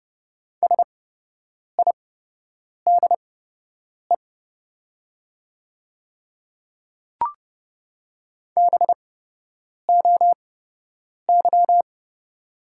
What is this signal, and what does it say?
Morse code, heard as a keyed tone, about 30 wpm, spaced as for 8 wpm, 700 Hz. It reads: SIDE BOY
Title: 8x Character Spacing - Top 200 Words - 30wpm